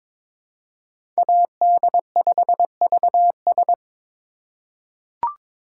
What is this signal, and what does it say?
Morse code, heard as a keyed tone, about 22 wpm, 700 Hz. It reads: AD5VS
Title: USA Call Signs - 5 Characters - 3x Character Spacing - 22wpm